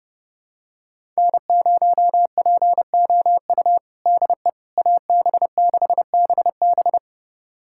Morse code, heard as a keyed tone, about 30 wpm, 700 Hz. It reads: N0POU DE AB6BB
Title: USA Callsign DE Callsign - 5 Characters - 30wpm